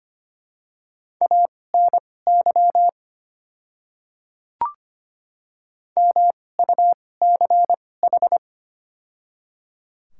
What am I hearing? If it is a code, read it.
Morse code, heard as a keyed tone, about 25 wpm, spaced as for 18 wpm, 700 Hz. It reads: ANY MUCH
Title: 2x Character Spacing - Top 200 Words - 25wpm